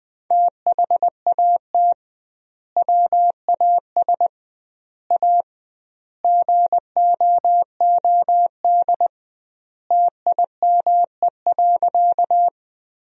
Morse code, heard as a keyed tone, about 20 wpm, 700 Hz. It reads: THAT WAS A GOOD TIME.
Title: 2x Word-Spacing - Sentences from Top 200 Words - 20wpm